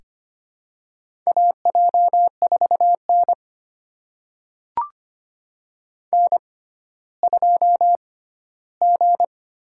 Morse code, heard as a keyed tone, about 25 wpm, spaced as for 14 wpm, 700 Hz. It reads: AJ4N N2G
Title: USA Call Signs - 6x Character Spacing - 25wpm